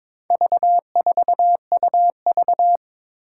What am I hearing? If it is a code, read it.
Morse code, heard as a keyed tone, about 22 wpm, 700 Hz. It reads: V4UV
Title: Four Challenging Letter-Number Combinations 22wpm